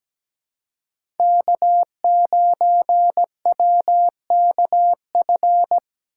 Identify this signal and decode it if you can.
Morse code, heard as a keyed tone, about 17 wpm, 700 Hz. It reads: K9WKF
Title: USA Call Signs - 5 Characters - 3x Character Spacing - 17wpm